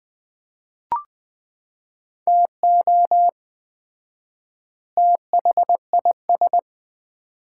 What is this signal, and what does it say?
Morse code, heard as a keyed tone, about 20 wpm, 700 Hz. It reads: TO THIS
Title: 4x Word-Spacing - Sets of 2 Words - 20wpm